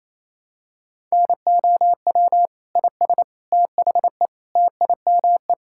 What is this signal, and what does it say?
Morse code, heard as a keyed tone, about 28 wpm, 700 Hz. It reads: NOW IS THE TIME
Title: Pangram Challenge v2 - 28wpm